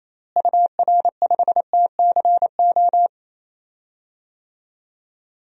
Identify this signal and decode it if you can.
Morse code, heard as a keyed tone, about 28 wpm, 700 Hz. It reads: UR5TCO
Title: DXCC-100 Call Signs - 6 Characters - Repeated 2x - 28wpm